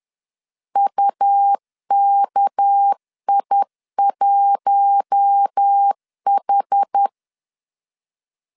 Morse code, heard as a keyed tone, about 10 wpm, 800 Hz. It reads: UKI1H
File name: CW / DET / 1 9 1500C S A 10wpm 4511